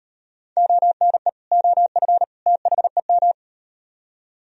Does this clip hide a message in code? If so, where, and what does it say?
Yes: Morse code from 0.6 to 3.3 s, heard as a keyed tone, about 38 wpm, 700 Hz. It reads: ONE OF THEM